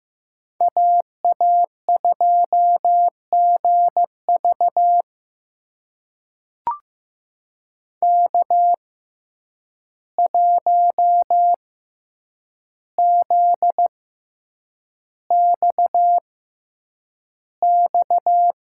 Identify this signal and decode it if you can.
Morse code, heard as a keyed tone, about 15 wpm, spaced as for 8 wpm, 700 Hz. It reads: AA2GV K1ZXX
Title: USA Call Signs - 5 Characters - 6x Character Spacing - 15wpm